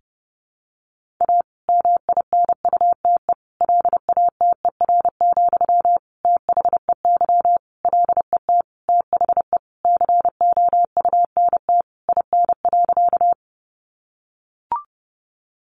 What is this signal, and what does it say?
Morse code, heard as a keyed tone, about 30 wpm, 700 Hz. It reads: A MINUTE LATER, THEY LET THE COUNT IN.
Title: Sentences from Top 1500 Words 30wpm